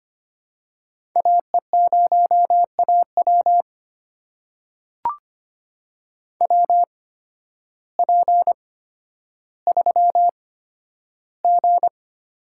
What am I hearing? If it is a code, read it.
Morse code, heard as a keyed tone, about 25 wpm, spaced as for 12 wpm, 700 Hz. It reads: AE0AW WP3G
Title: USA Call Signs - 5 Characters - 8x Character Spacing - 25wpm